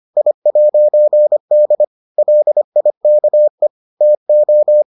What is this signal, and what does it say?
Morse code, heard as a keyed tone, about 25 wpm, 600 Hz. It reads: I'D LIKE TO